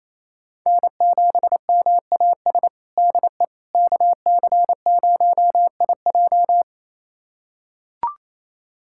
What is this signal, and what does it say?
Morse code, heard as a keyed tone, about 27 wpm, 700 Hz. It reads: N7MAS DE KC0IJ
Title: USA Callsign DE Callsign - 5 Characters - 28wpm